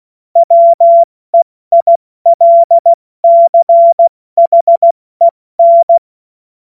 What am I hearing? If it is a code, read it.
Morse code, heard as a keyed tone, about 16 wpm, 680 Hz. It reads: WEILCHEN
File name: Morsekurs / Witze / Witz 17. August 2021, 70 F80